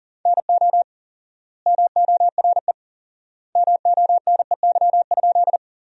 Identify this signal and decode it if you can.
Morse code, heard as a keyed tone, about 40 wpm, 700 Hz. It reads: NO MORE MONEY?
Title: 4x Word-Spacing - Sentences from Top 400 Words - 40wpm